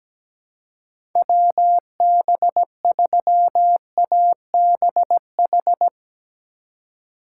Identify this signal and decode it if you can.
Morse code, heard as a keyed tone, about 17 wpm, 700 Hz. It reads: WB3ABH
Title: USA Call Signs - 4x Character Spacing - 17wpm